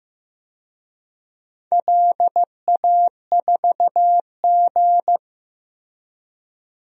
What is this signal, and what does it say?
Morse code, heard as a keyed tone, about 15 wpm, 700 Hz. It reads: LA4G
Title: DXCC-100 Call Signs - 4 Characters - Repeated 2x - 15wpm